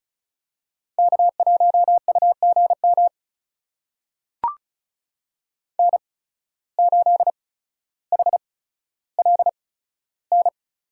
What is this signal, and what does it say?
Morse code, heard as a keyed tone, about 36 wpm, spaced as for 15 wpm, 700 Hz. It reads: K1UGM N8HLN
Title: USA Call Signs - 5 Characters - 8x Character Spacing - 35wpm